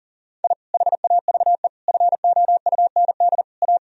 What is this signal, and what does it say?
Morse code, heard as a keyed tone, about 40 wpm, 700 Hz. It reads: I HAVE FOUND A
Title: Sentences from Top 1000 Words - Review Part 3 - 40wpm